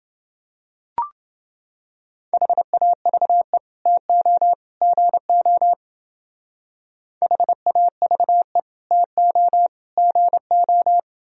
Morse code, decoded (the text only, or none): HAVE TO GO HAVE TO GO